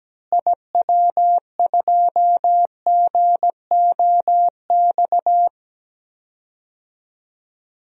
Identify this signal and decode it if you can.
Morse code, heard as a keyed tone, about 17 wpm, 700 Hz. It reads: IW2GOX
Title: DXCC-100 Call Signs - 6 Characters - Repeated 2x - 17wpm